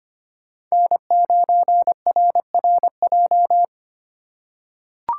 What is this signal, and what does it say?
Morse code, heard as a keyed tone, about 25 wpm, 700 Hz. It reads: N9RRJ
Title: USA Call Signs - 3x Character Spacing - 25wpm